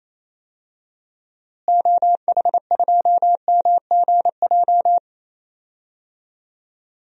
Morse code, heard as a keyed tone, about 28 wpm, 700 Hz. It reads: OH2MGJ
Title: DXCC-100 Call Signs - 6 Characters - Repeated 3x - 28wpm